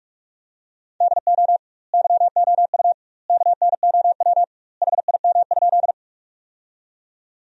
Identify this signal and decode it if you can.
Morse code, heard as a keyed tone, about 46 wpm, 700 Hz. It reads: DO YOU KNOW HIM?
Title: 2x Word-Spacing - Sentences from Top 100 Words - 45wpm